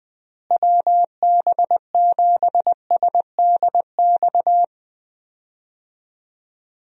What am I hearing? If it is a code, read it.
Morse code, heard as a keyed tone, about 20 wpm, 700 Hz. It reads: WB7SDX
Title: USA Call Signs - 6 Characters - Repeated 3x - 20wpm